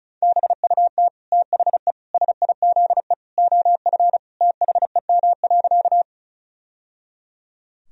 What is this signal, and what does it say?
Morse code, heard as a keyed tone, about 34 wpm, 700 Hz. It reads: BUT THE SIZE OF THEM.
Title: Sentences from Top 1000 Words - Review Part 2 - 35wpm